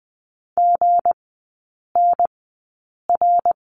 Morse code, heard as a keyed tone, about 20 wpm, spaced as for 8 wpm, 700 Hz. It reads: GNR
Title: 159 Lesson 26e - Letter g - Three Characters 20wpm